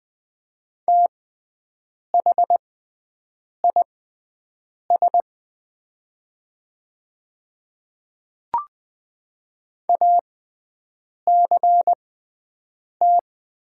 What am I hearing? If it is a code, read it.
Morse code, heard as a keyed tone, about 20 wpm, spaced as for 7 wpm, 700 Hz. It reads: THIS ACT
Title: 6x Character Spacing - Top 200 Words - 20wpm